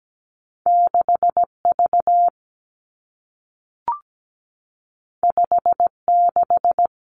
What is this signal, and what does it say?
Morse code, heard as a keyed tone, about 17 wpm, 700 Hz. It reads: 6V 56
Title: Two Challenging Letter-Number Combinations 17wpm